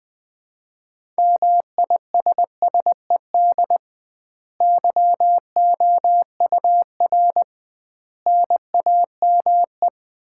Morse code, heard as a keyed tone, about 20 wpm, 700 Hz. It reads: MISSED YOUR NAME
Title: Common QSO Phrases - 2x Word-Spacing - 20wpm